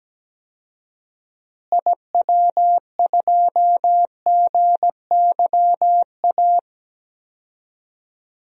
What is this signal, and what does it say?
Morse code, heard as a keyed tone, about 17 wpm, 700 Hz. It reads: IW2GYA
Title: DXCC-100 Call Signs - 6 Characters - Repeated 2x - 17wpm